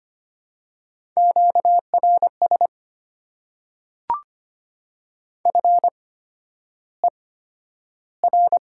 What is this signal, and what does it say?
Morse code, heard as a keyed tone, about 25 wpm, spaced as for 11 wpm, 690 Hz. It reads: QRS FER
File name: Top 90 QSO Elements - 8x Character Spacing - 25wpm